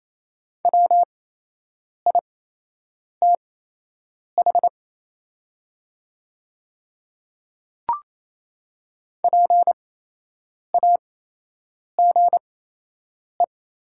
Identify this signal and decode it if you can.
Morse code, heard as a keyed tone, about 28 wpm, spaced as for 8 wpm, 700 Hz. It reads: WITH PAGE
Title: 8x Character Spacing - Top 500 Words - 28wpm